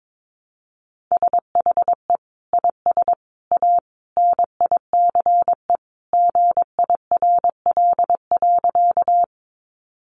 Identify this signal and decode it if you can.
Morse code, heard as a keyed tone, about 22 wpm, 700 Hz. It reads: SHE IS A NICE GIRL.
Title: Sentences from Top 1100 Words 22wpm